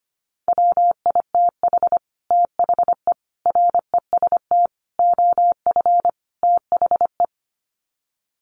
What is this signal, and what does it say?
Morse code, heard as a keyed tone, about 25 wpm, 700 Hz. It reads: WITH THE REST OF THE
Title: Sets of 5 Words 25wpm